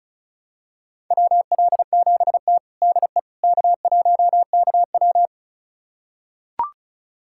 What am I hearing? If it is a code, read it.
Morse code, heard as a keyed tone, about 36 wpm, 700 Hz. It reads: WL7T DE K1KW
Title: USA Callsign DE Callsign - 4 Characters - 35wpm